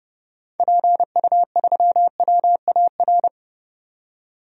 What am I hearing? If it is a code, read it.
Morse code, heard as a keyed tone, about 30 wpm, 700 Hz. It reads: PU3WAR
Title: DXCC-100 Call Signs - 6 Characters - 30wpm